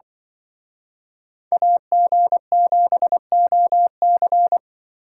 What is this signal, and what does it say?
Morse code, heard as a keyed tone, about 25 wpm, 700 Hz. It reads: AG7OC
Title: Speed Racing - USA 2x2 Call Signs 20wpm